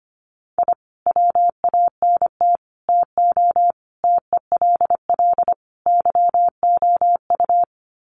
Morse code, heard as a keyed tone, about 25 wpm, 700 Hz. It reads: I WANT TO TELL YOU
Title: Sets of 5 Words Encore 25wpm